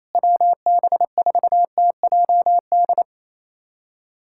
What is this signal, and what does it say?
Morse code, heard as a keyed tone, about 28 wpm, 700 Hz. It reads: WB4TJD
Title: USA Callsign DE Callsign 28wpm